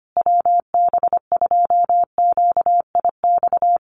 Morse code, heard as a keyed tone, about 25 wpm, 700 Hz. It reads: WB2QIX
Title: Common QSO Phrases Encore 25wpm